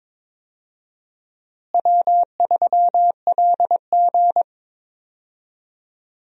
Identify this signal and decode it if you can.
Morse code, heard as a keyed tone, about 21 wpm, 700 Hz. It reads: W3LG
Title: USA Call Signs - 4 Characters - Repeated 3x - 22wpm